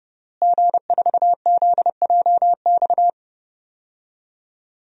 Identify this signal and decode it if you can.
Morse code, heard as a keyed tone, about 30 wpm, 700 Hz. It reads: G4ZJX
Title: DXCC-100 Call Signs - 5 Characters - Repeated 3x - 30wpm